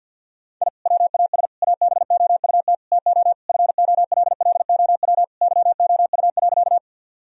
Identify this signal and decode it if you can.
Morse code, heard as a keyed tone, about 52 wpm, 700 Hz. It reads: I WAS ABOUT TO FOLLOW YOU.
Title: Sentences from Top 1000 Words - Review Part 1 - 50wpm